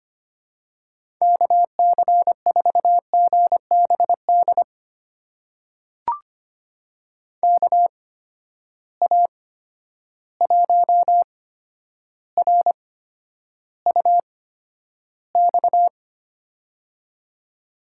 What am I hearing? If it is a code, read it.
Morse code, heard as a keyed tone, about 25 wpm, spaced as for 11 wpm, 700 Hz. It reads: KC4GBD KA1RUX
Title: USA Call Signs - 6 Characters - 8x Character Spacing - 25wpm